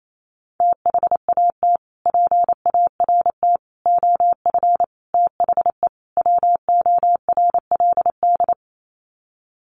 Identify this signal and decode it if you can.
Morse code, heard as a keyed tone, about 28 wpm, 700 Hz. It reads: THAT PART OF THE WORLD
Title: Sets of 5 Words Encore2 28wpm